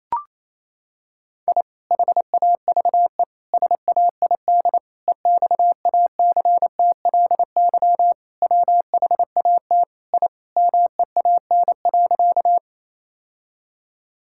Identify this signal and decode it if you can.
Morse code, heard as a keyed tone, about 27 wpm, 700 Hz. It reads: I HAVE SAID EXACTLY WHAT I MEAN.
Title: Sentences from Top 1000 Words - Review Part 6 - 28wpm